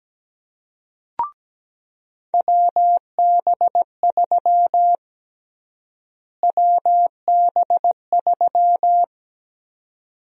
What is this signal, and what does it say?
Morse code, heard as a keyed tone, about 17 wpm, 700 Hz. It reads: WB3 WB3
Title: USA Call Sign Prefixes - Repeated 2x - 17wpm